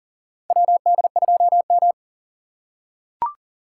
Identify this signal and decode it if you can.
Morse code, heard as a keyed tone, about 40 wpm, 700 Hz. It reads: WD2M
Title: USA Call Signs - 4 Characters - 4x Character Spacing - 40wpm